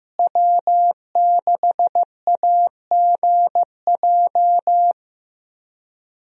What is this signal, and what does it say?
Morse code, heard as a keyed tone, about 16 wpm, 700 Hz. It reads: W6AGJ
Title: USA Call Signs - 5 Characters - Repeated 3x - 15wpm